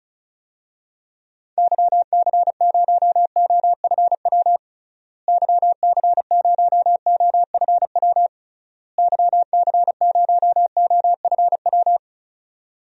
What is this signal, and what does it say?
Morse code, heard as a keyed tone, about 34 wpm, 700 Hz. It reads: YC0OFW YC0OFW YC0OFW
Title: DXCC-100 Call Signs - 6 Characters - Repeated 3x - 35wpm